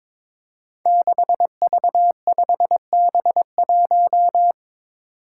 Morse code, heard as a keyed tone, about 22 wpm, 700 Hz. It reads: 6V5B1
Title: Five Challenging Letter-Number Combinations 22wpm